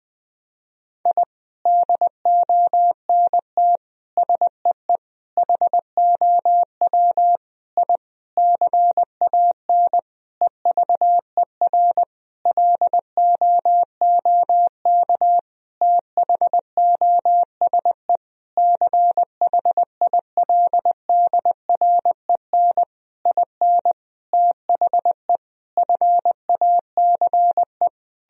Morse code, heard as a keyed tone, about 21 wpm, 700 Hz. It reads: I DONT SEE HOW I CAN EVER LOOK THOSE CHILDREN IN THE FACE
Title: Sentences from Top 1000 Words - Review Part 2 - 20wpm